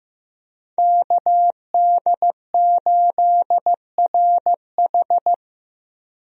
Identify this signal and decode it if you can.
Morse code, heard as a keyed tone, about 15 wpm, 700 Hz. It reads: KD8RH
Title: USA Call Signs - 5 Characters - Repeated 2x - 15wpm